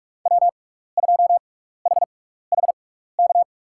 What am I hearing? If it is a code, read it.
Morse code, heard as a keyed tone, about 47 wpm, spaced as for 16 wpm, 700 Hz. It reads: W2HHK
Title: USA Call Signs - 6x Character Spacing - 45wpm